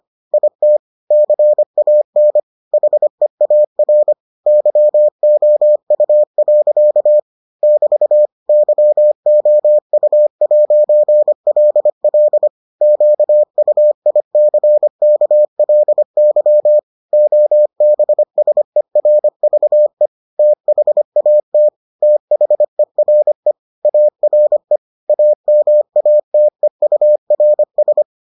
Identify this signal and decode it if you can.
Morse code, heard as a keyed tone, about 25 wpm, 600 Hz. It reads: IT CAN HEAR YOU. <BT> YOU'LL QUICKLY OBSERVE THAT THERE ARE AMATEURS